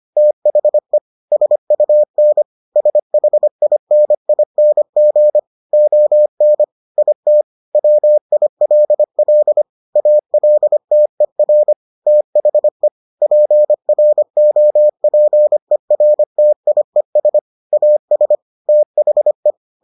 Morse code, heard as a keyed tone, about 25 wpm, 600 Hz. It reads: THE SUN SHINING ON IT WILL ALTER THE PROPERTIES AS THE